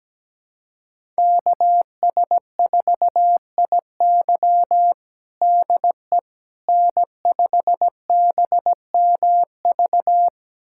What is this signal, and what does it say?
Morse code, heard as a keyed tone, about 17 wpm, 700 Hz. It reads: KS4IY DE N5BMV
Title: USA Callsign DE Callsign - 5 Characters - 17wpm